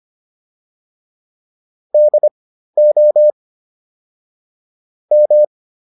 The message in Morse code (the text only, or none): DO M